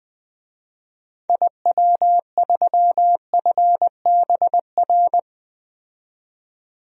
Morse code, heard as a keyed tone, about 20 wpm, 700 Hz. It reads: IW3FBR
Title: DXCC-100 Call Signs - 6 Characters - 20wpm